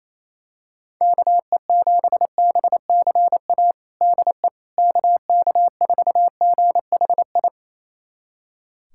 Morse code, heard as a keyed tone, about 28 wpm, 700 Hz. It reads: KE7BCA DE KK4GHI
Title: USA Callsign DE Callsign 28wpm